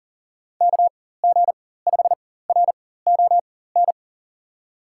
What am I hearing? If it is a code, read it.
Morse code, heard as a keyed tone, about 40 wpm, spaced as for 19 wpm, 700 Hz. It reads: KG5RON
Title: USA Call Signs - 6 Characters - 4x Character Spacing - 40wpm